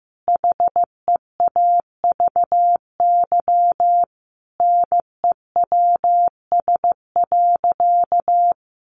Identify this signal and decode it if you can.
Morse code, heard as a keyed tone, about 14 wpm, 700 Hz. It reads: HEAVY NEWS.
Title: Sentences from Top 1100 Words 15wpm